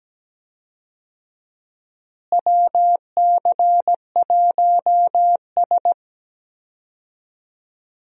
Morse code, heard as a keyed tone, about 17 wpm, 700 Hz. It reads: WC1S
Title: USA Call Signs - 4 Characters - Repeated 3x - 17wpm